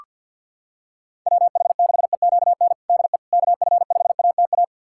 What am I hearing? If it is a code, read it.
Morse code, heard as a keyed tone, about 47 wpm, 700 Hz. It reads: WH6EQN DE KF5ATA
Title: USA Callsign DE Callsign - 6 Characters - 50wpm